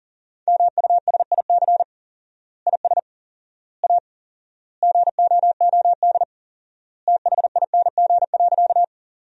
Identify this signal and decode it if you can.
Morse code, heard as a keyed tone, about 39 wpm, 700 Hz. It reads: MUSIC IS A GOOD THING.
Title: 4x Word-Spacing - Sentences from Top 300 Words - 40wpm